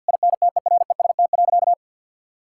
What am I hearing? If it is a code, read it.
Morse code, heard as a keyed tone, about 50 wpm, 700 Hz. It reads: INTEREST.
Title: Sentences from Top 1000 Words - Review Part 6 - 50wpm